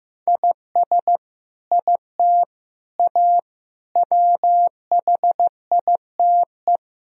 Morse code, heard as a keyed tone, about 15 wpm, 700 Hz. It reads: IS IT A WHITE
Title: Sentences from Top 1000 Words - Review Part 1 - 15wpm